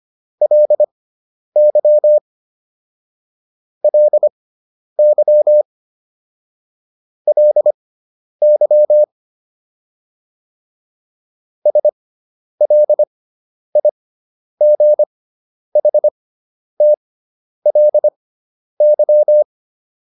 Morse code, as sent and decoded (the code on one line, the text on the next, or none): .-.. -.-- / .-.. -.-- / .-.. -.-- / ... .-.. .. --. .... - .-.. -.--
LY LY LY SLIGHTLY